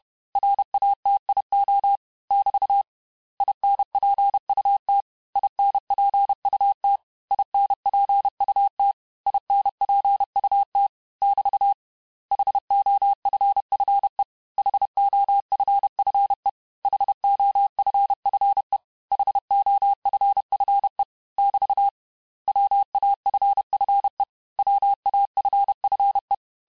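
Morse code, heard as a keyed tone, about 31 wpm, 800 Hz. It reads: RATIO <BT> INPUT INPUT INPUT INPUT <BT> HOFFE HOFFE HOFFE HOFFE <BT> WAFFE WAFFE
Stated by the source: Morse lesson of words, each word repeated 4 times, at 29 words per minute.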